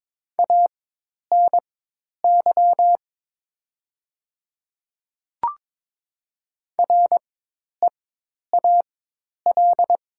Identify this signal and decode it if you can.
Morse code, heard as a keyed tone, about 22 wpm, spaced as for 10 wpm, 700 Hz. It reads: ANY REAL
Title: 4x Character Spacing - Top 500 Words - 22wpm